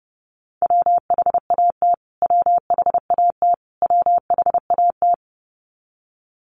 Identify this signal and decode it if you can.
Morse code, heard as a keyed tone, about 30 wpm, 700 Hz. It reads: WHAT WHAT WHAT
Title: Top 200 Words Repeated 30wpm